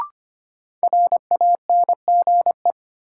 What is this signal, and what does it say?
Morse code, heard as a keyed tone, about 25 wpm, 700 Hz. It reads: RANGE
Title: Sets of 4 Words - New Word Drill - 25wpm